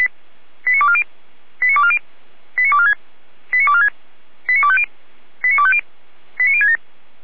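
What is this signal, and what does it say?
Five-tone selective call (ZVEI), heard as quick runs of five tones, tones of about 70 ms. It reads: 89240 89240 89246 89246 89250 89250 89067
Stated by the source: ZVEI five-tone call 89160 received by SDR, fire brigade Feuerwehr Orlamünde.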